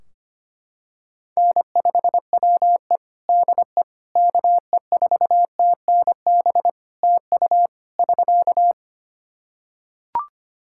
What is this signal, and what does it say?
Morse code, heard as a keyed tone, about 25 wpm, 700 Hz. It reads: N5WE DE KE4TNB TU <SK>
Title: K1USN SST QSO Practice 25wpm